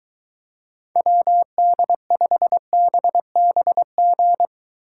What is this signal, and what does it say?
Morse code, heard as a keyed tone, about 23 wpm, 700 Hz. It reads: WD5BBG